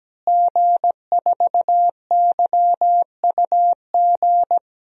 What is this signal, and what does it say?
Morse code, heard as a keyed tone, about 17 wpm, 700 Hz. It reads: G4YUG
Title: DXCC-100 Call Signs - 5 Characters - Repeated 3x - 17wpm